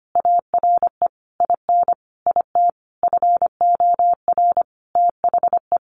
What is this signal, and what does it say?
Morse code, heard as a keyed tone, about 25 wpm, 700 Hz. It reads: ARE IN IT FOR THE
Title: Sentences from Top 1300 Words 25wpm